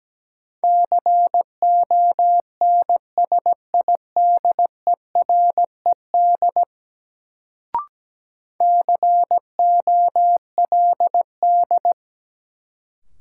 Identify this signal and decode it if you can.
Morse code, heard as a keyed tone, about 17 wpm, 700 Hz. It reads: CONSIDERED COLD
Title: Sets of 5 Words - New Word Drill - 17wpm